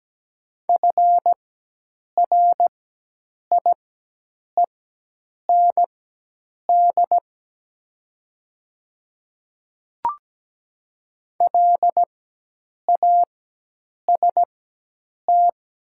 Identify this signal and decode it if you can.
Morse code, heard as a keyed tone, about 17 wpm, spaced as for 8 wpm, 700 Hz. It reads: FRIEND LAST
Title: 4x Character Spacing - Top 300 Words - 17wpm